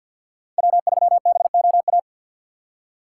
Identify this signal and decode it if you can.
Morse code, heard as a keyed tone, about 62 wpm, 700 Hz. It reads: W3BOA
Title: Speed Racing - USA 1x3 Call Signs 50wpm